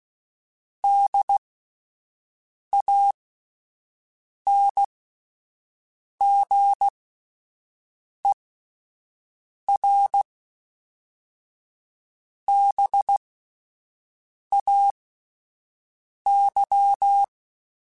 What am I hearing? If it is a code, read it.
Morse code, heard as a keyed tone, about 16 wpm, spaced as for 5 wpm, 800 Hz. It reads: DANGERBAY